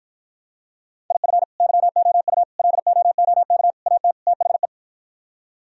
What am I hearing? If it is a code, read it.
Morse code, heard as a keyed tone, about 52 wpm, 700 Hz. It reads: IF YOU LOOK AT THE